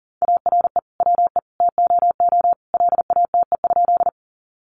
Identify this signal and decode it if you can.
Morse code, heard as a keyed tone, about 40 wpm, 700 Hz. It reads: ARE WE TOO LATE?
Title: Sentences from Top 300 Words 40wpm